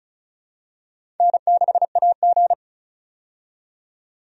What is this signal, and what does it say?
Morse code, heard as a keyed tone, about 36 wpm, 700 Hz. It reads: N6AG
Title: USA Call Signs - 4 Characters - 35wpm